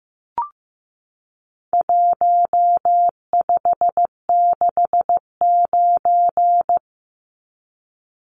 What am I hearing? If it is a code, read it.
Morse code, heard as a keyed tone, about 15 wpm, 700 Hz. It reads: 1569
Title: Numbers - Contest Style - 4-Digit 15wpm